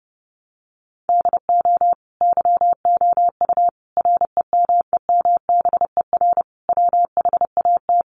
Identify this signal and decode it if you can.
Morse code, heard as a keyed tone, about 30 wpm, 700 Hz. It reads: DO YOU REMEMBER WHAT
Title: Sentences from Top 700 Words 30wpm